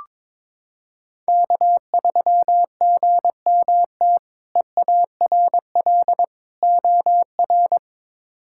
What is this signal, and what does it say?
Morse code, heard as a keyed tone, about 22 wpm, 700 Hz. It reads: K3GMT EARL OR
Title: K1USN SST QSO Practice - 22wpm